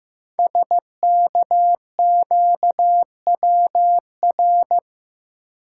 Morse code, heard as a keyed tone, about 15 wpm, 700 Hz. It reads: SKQWR
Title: Code Groups - 5 Letter:Numbers - 15wpm